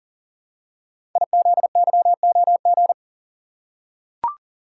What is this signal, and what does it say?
Morse code, heard as a keyed tone, about 38 wpm, 700 Hz. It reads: IZYOG
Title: Code Groups - 5 Letters - 40wpm